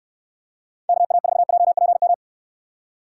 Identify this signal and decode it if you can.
Morse code, heard as a keyed tone, about 69 wpm, 700 Hz. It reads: KA4JPM